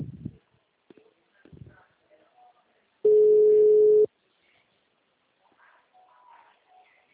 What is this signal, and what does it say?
Ringback tone (Europe, 425 Hz), with speech.
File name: out- -2005- - - .357